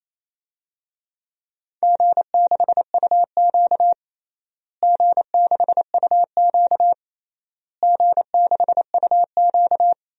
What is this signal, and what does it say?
Morse code, heard as a keyed tone, about 28 wpm, 700 Hz. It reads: G6UQ G6UQ G6UQ
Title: DXCC-100 Call Signs - 4 Characters - Repeated 3x - 28wpm